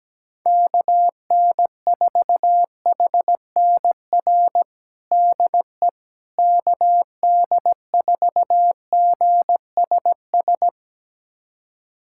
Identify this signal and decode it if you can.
Morse code, heard as a keyed tone, about 17 wpm, 700 Hz. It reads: KN4HNR DE KD4GSS
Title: USA Callsign DE Callsign - 6 Characters - 17wpm